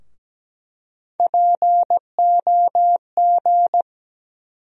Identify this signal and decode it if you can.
Morse code, heard as a keyed tone, about 17 wpm, 700 Hz. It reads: POG
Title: USA Call Sign Postfixes - 3 Letters 17wpm